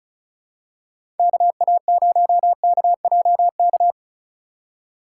Morse code, heard as a keyed tone, about 36 wpm, 700 Hz. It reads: KA0KJK
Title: USA Call Signs - 6 Characters - 2x Character Spacing - 35wpm